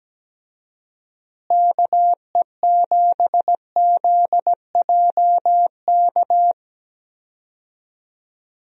Morse code, heard as a keyed tone, about 17 wpm, 700 Hz. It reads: KE7ZJK
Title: USA Call Signs - 6 Characters - Repeated 2x - 17wpm